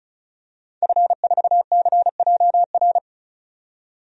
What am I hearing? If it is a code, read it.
Morse code, heard as a keyed tone, about 36 wpm, 700 Hz. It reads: F4CJR
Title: DXCC-100 Call Signs - 5 Characters - 35wpm